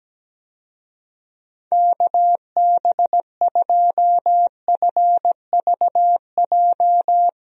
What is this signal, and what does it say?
Morse code, heard as a keyed tone, about 17 wpm, 700 Hz. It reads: KB2FVJ
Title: USA Call Signs - 6 Characters - Repeated 2x - 17wpm